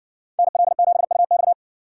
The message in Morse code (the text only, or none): NL7UX